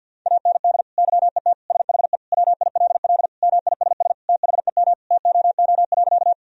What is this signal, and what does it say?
Morse code, heard as a keyed tone, about 50 wpm, 700 Hz. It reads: AND YET SHE WILL MISS THEM TOO.
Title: Sentences from Top 1000 Words - Review Part 3 - 50wpm